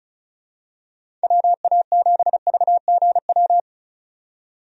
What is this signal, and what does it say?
Morse code, heard as a keyed tone, about 34 wpm, 700 Hz. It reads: WA7VGW
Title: USA Call Signs - 6 Characters - 35wpm